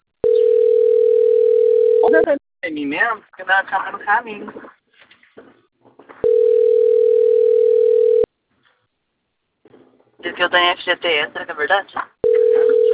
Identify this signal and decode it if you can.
Ringback tone (North America), 440 + 480 Hz, with speech.
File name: out- - - - - .45700